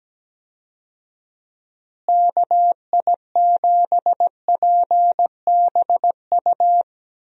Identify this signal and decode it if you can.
Morse code, heard as a keyed tone, about 17 wpm, 700 Hz. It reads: KI7PBU
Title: USA Call Signs - 6 Characters - Repeated 3x - 17wpm